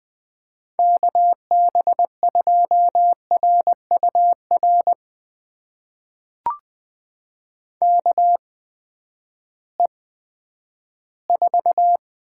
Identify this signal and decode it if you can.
Morse code, heard as a keyed tone, about 21 wpm, spaced as for 12 wpm, 700 Hz. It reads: KB2RUR KE4
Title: USA Call Signs - 6 Characters - 8x Character Spacing - 20wpm